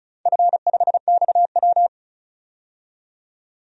Morse code, heard as a keyed tone, about 36 wpm, 700 Hz. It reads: F5XW
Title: DXCC-100 Call Signs - 4 Characters - 35wpm